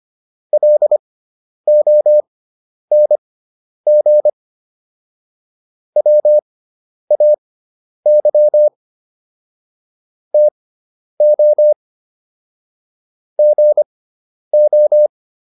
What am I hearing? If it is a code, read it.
Morse code, heard as a keyed tone, about 25 wpm, spaced as for 10 wpm, 600 Hz. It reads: LONG WAY TO GO